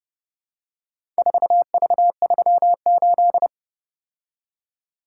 Morse code, heard as a keyed tone, about 30 wpm, 700 Hz. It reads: 4V38